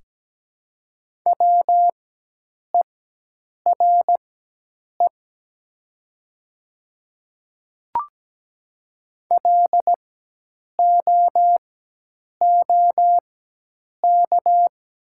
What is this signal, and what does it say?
Morse code, heard as a keyed tone, about 17 wpm, spaced as for 8 wpm, 700 Hz. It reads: WERE LOOK
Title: 4x Character Spacing - Top 100 Words - 17wpm